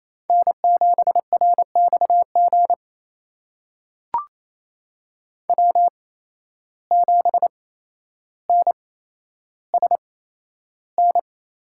Morse code, heard as a keyed tone, about 28 wpm, spaced as for 12 wpm, 700 Hz. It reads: N7RXG W7NSN
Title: USA Call Signs - 5 Characters - 8x Character Spacing - 28wpm